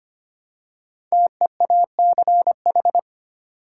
Morse code, heard as a keyed tone, about 25 wpm, 700 Hz. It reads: TEACH
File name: 5 Letter Words - First 3 Letters Given - 25wpm